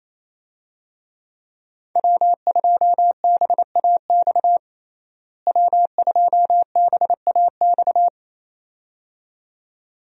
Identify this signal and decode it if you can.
Morse code, heard as a keyed tone, about 28 wpm, 700 Hz. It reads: W2BAX W2BAX
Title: USA Call Signs - 5 Characters - Repeated 2x - 28wpm